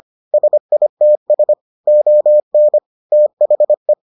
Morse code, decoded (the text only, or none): SITS ON THE